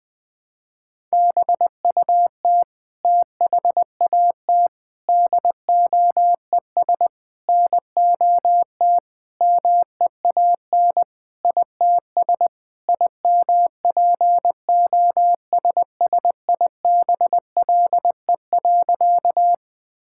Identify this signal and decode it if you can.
Morse code, heard as a keyed tone, about 20 wpm, 700 Hz. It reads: BUT THAT DOES NOT MEAN ITS IMPOSSIBLE.
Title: Sentences with 5-Word Phrases - 20wpm